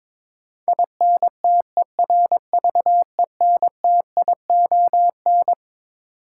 Morse code, heard as a keyed tone, about 22 wpm, 700 Hz. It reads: INTERVENTION
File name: Twelve Letter Words - 22wpm